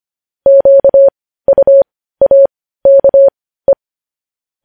Morse code, heard as a keyed tone, about 25 wpm, spaced as for 15 wpm, 550 Hz. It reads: QUAKE
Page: news-headlines-in-morse-code-at-15-wpm